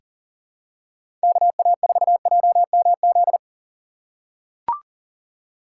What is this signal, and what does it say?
Morse code, heard as a keyed tone, about 40 wpm, 700 Hz. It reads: KA4JMZ
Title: USA Call Signs - 6 Characters - 4x Character Spacing - 40wpm